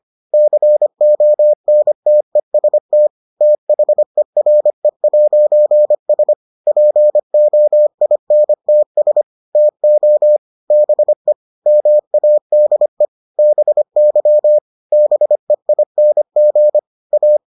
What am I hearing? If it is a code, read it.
Morse code, heard as a keyed tone, about 25 wpm, 600 Hz. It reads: CONTEST THERE'S POINTS TO BE MADE BY BEING A